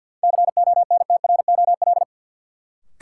Morse code, heard as a keyed tone, about 50 wpm, 700 Hz. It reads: CONTROL
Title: Sentences from Top 1000 Words - Review Part 2 - New Word Drill - 50wpm